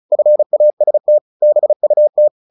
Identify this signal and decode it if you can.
Morse code, heard as a keyed tone, about 34 wpm, 600 Hz. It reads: FAST BUT
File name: the old man and the sea / the old man and the sea 1 35wpm